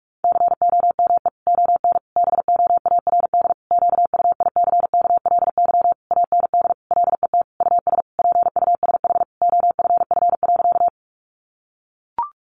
Morse code, heard as a keyed tone, about 44 wpm, 700 Hz. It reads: COME ON BOARD QUICKLY AND LET US PUSH OFF.
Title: Sentences from Top 1300 Words 45wpm